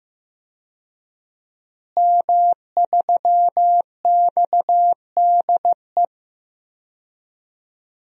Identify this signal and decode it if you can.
Morse code, heard as a keyed tone, about 15 wpm, 700 Hz. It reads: M3XDE
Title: DXCC-100 Call Signs - 5 Characters - Repeated 2x - 15wpm